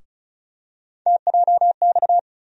Morse code, heard as a keyed tone, about 36 wpm, 700 Hz. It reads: TJX
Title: USA Call Sign Postfixes - 3 Letters 35wpm